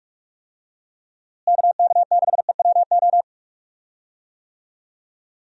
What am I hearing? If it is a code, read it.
Morse code, heard as a keyed tone, about 46 wpm, 700 Hz. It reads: KK6EWO
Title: USA Call Signs - 6 Characters - Repeated 2x - 45wpm